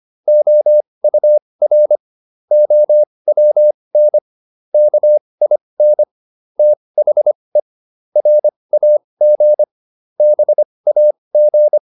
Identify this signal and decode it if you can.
Morse code, heard as a keyed tone, about 25 wpm, spaced as for 20 wpm, 600 Hz. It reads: OUR OWN KIN THE RAG BAG